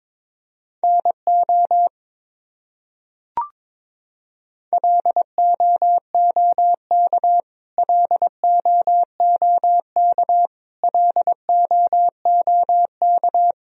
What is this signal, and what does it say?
Morse code, heard as a keyed tone, about 22 wpm, 700 Hz. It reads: NO LOOK LOOK LOOK
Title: Top 100 Words Repeated 22wpm